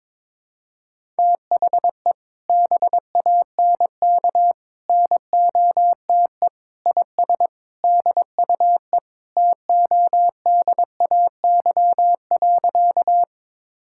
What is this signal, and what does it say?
Morse code, heard as a keyed tone, about 21 wpm, 700 Hz. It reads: THE BANK NOTE IS DUE TODAY.
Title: Sentences from Top 1000 Words - Review Part 6 - 22wpm